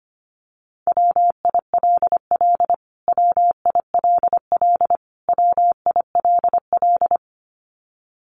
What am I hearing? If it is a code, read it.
Morse code, heard as a keyed tone, about 25 wpm, 700 Hz. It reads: WILL WILL WILL
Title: Top 200 Words Repeated 25wpm